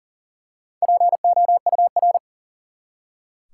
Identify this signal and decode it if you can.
Morse code, heard as a keyed tone, about 40 wpm, 700 Hz. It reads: POUR
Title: Four Letter Words - 40wpm